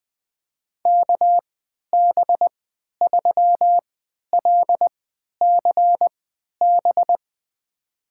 Morse code, heard as a keyed tone, about 21 wpm, spaced as for 12 wpm, 700 Hz. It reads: KB3LCB
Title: USA Call Signs - 6 Characters - 3x Character Spacing - 20wpm